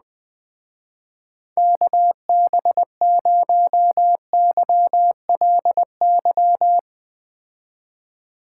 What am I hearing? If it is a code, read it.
Morse code, heard as a keyed tone, about 19 wpm, 700 Hz. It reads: KB0YLY
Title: Speed Racing - USA 2x3 Call Signs 17wpm